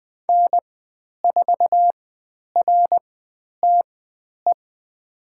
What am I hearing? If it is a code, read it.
Morse code, heard as a keyed tone, about 20 wpm, spaced as for 10 wpm, 700 Hz. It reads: N4RTE
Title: 102 Lesson 17g - Letter u - Call Signs 20wpm